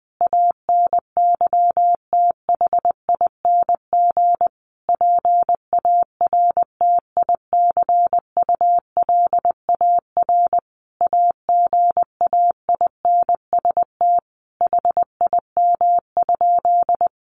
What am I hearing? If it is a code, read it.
Morse code, heard as a keyed tone, about 20 wpm, 700 Hz. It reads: ANYTHING PARTICULAR AGAINST HIM?
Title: Sentences from Top 1000 Words 20wpm